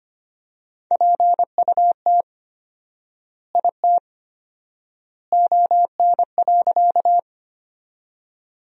Morse code, heard as a keyed tone, about 25 wpm, 700 Hz. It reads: PUT IT ON.
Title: 4x Word-Spacing - Sentences from Top 200 Words - 25wpm